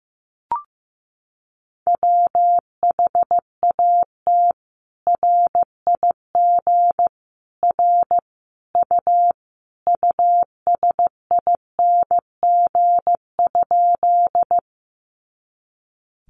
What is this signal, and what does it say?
Morse code, heard as a keyed tone, about 15 wpm, 700 Hz. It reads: WHAT RIG R U USING?
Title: Common QSO Phrases 15wpm